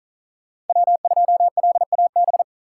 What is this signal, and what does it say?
Morse code, heard as a keyed tone, about 46 wpm, 700 Hz. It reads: W2LAB